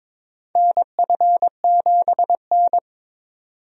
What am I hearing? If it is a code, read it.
Morse code, heard as a keyed tone, about 22 wpm, 700 Hz. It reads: NF7N
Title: USA Call Signs - 4 Characters - 22wpm